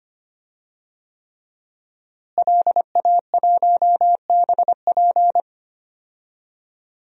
Morse code, heard as a keyed tone, about 25 wpm, 700 Hz. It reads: LA1BP
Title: DXCC-100 Call Signs - 5 Characters - Repeated 3x - 25wpm